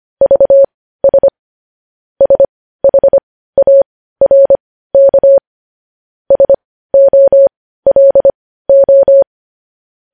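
Morse code, heard as a keyed tone, about 25 wpm, spaced as for 15 wpm, 550 Hz. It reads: VS SHARK SOLO
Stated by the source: Morse practice sending news headlines at 15 wpm.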